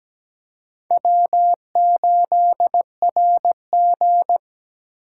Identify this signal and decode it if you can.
Morse code, heard as a keyed tone, about 17 wpm, 700 Hz. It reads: W8RG
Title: USA Call Signs - 4 Characters - 4x Character Spacing - 17wpm